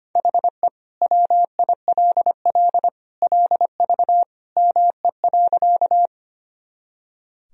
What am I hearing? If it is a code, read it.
Morse code, heard as a keyed tone, about 25 wpm, 700 Hz. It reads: HE WILL LV ME.
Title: Sentences from Top 400 Words and Abbreviations 25wpm